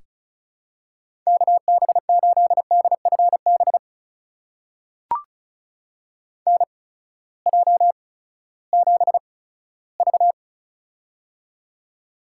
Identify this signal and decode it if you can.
Morse code, heard as a keyed tone, about 36 wpm, spaced as for 18 wpm, 700 Hz. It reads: KB8DFB NJ7V
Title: USA Call Signs - 8x Character Spacing - 35wpm